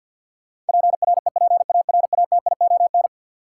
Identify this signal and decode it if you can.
Morse code, heard as a keyed tone, about 51 wpm, 700 Hz. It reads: PREPARATION
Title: Sentences from Top 1400 Words - New Word Drill Alt - 50wpm